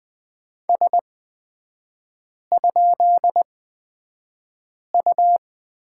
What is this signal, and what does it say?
Morse code, heard as a keyed tone, about 20 wpm, spaced as for 5 wpm, 700 Hz. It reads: S?U
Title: 236 Lesson 38e - Letter q - Three Characters 20wpm